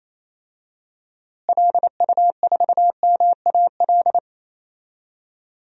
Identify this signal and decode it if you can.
Morse code, heard as a keyed tone, about 28 wpm, 700 Hz. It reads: LU4MAL
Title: DXCC-100 Call Signs - 6 Characters - Repeated 3x - 28wpm